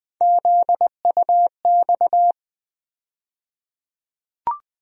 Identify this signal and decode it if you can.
Morse code, heard as a keyed tone, about 21 wpm, 700 Hz. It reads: ZUX
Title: USA Call Sign Postfixes - 3 Letters 20wpm